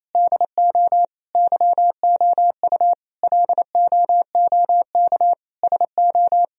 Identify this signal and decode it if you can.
Morse code, heard as a keyed tone, about 28 wpm, 700 Hz. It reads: DO YOU LOOK SO
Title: Sentences from Top 1000 Words - Review Part 1 - 28wpm